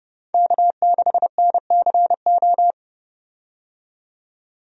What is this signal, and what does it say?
Morse code, heard as a keyed tone, about 31 wpm, 700 Hz. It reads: K6NCO
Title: USA Call Signs - 5 Characters - 30wpm